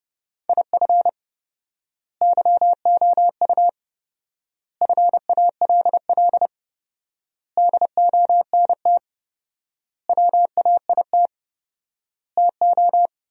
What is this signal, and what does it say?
Morse code, heard as a keyed tone, about 30 wpm, 700 Hz. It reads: IF YOU FALL DONT WAIT TO
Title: 4x Word-Spacing - Sentences from Top 500 Words - 30wpm